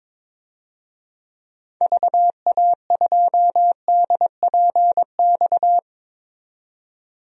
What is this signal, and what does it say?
Morse code, heard as a keyed tone, about 22 wpm, 700 Hz. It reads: VA2DPX
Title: DXCC-100 Call Signs - 6 Characters - 22wpm